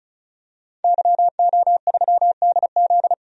Morse code, heard as a keyed tone, about 36 wpm, 700 Hz. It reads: YO3DZ
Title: DXCC-100 Call Signs - 5 Characters - 35wpm